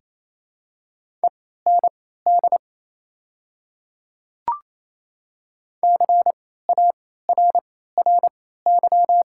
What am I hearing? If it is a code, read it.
Morse code, heard as a keyed tone, about 28 wpm, spaced as for 16 wpm, 700 Hz. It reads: END CARRY
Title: 3x Character Spacing - Top 300 Words - 28wpm